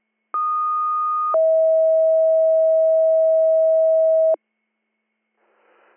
Two-tone page, heard as a one-second tone, then a three-second tone, about 1230 then 640 Hz. It is heard from 0.3 to 4.3 s.